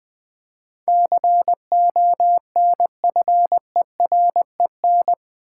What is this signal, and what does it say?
Morse code, heard as a keyed tone, about 21 wpm, 700 Hz. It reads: CONFEREN
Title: Build A Word - 4 Letters At A Time - 10-Letter Words - 20wpm